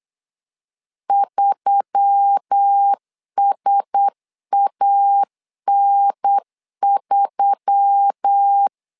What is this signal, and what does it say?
Morse code, heard as a keyed tone, about 8 wpm, 800 Hz. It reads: 3SAN3
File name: LEZIONI / 4 Q1S3 8wpm 3037